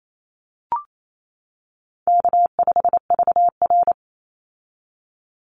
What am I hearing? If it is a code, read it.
Morse code, heard as a keyed tone, about 28 wpm, 700 Hz. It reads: K5VR
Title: Easy Call Signs - 28wpm